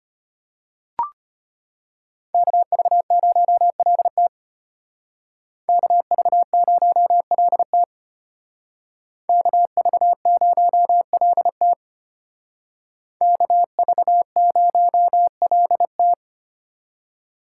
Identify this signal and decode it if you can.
Morse code, heard as a keyed tone, about 32 wpm, 700 Hz. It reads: KV0LT KV0LT KV0LT KV0LT